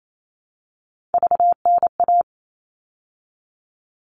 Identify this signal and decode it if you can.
Morse code, heard as a keyed tone, about 28 wpm, 700 Hz. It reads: VNA
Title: All Cut Numbers - 3-Digit 28wpm